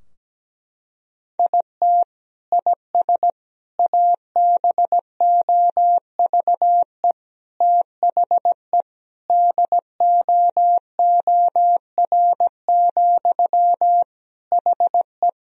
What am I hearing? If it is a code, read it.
Morse code, heard as a keyed tone, about 17 wpm, 700 Hz. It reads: IT IS ABOVE THE DOOR, HE
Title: Sentences from Top 1000 Words - Review Part 2 - 17wpm